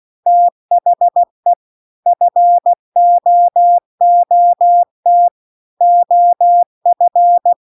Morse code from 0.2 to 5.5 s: THE FOOT